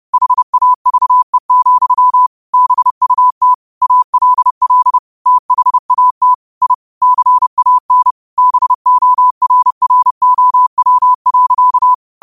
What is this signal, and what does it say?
Morse code, heard as a keyed tone, about 30 wpm, 1000 Hz. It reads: HAVE, BUT ALL THAT I CAN BORROW.
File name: qotdfiles / 30wpm